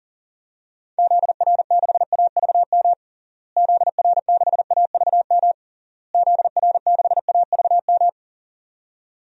Morse code, heard as a keyed tone, about 40 wpm, 700 Hz. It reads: ZR6AVM ZR6AVM ZR6AVM
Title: DXCC-100 Call Signs - 6 Characters - Repeated 3x - 40wpm